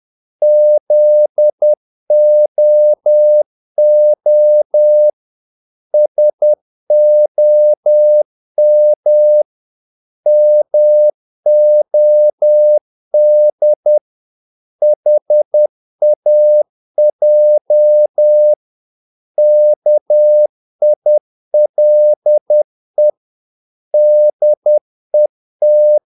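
Morse code, heard as a keyed tone, about 10 wpm, 600 Hz. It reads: ZOO SOM MOD HAJ KILE DET
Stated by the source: Morse practice of short Danish words at 10 wpm, from CW med Gnister.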